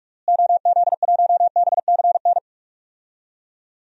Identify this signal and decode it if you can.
Morse code, heard as a keyed tone, about 46 wpm, 700 Hz. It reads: OZ1BCN
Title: DXCC-100 Call Signs - 6 Characters - Repeated 3x - 45wpm